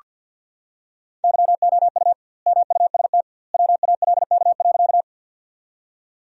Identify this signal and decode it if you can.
Morse code, heard as a keyed tone, about 49 wpm, 700 Hz. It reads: YOU MUST WALK.
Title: 2x Word-Spacing - Sentences from Top 300 Words - 50wpm